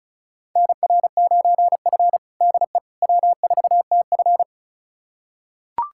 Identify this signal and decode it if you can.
Morse code, heard as a keyed tone, about 36 wpm, 700 Hz. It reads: NR9F DE W4TF
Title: USA Callsign DE Callsign - 4 Characters - 35wpm